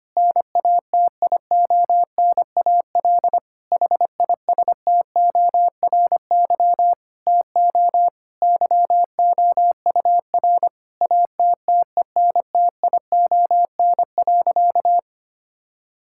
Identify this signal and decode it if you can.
Morse code, heard as a keyed tone, about 25 wpm, 700 Hz. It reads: NATIONAL HISTORY TO YOUR ATTENTION.
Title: Sentences from Top 1000 Words - Review Part 6 - 25wpm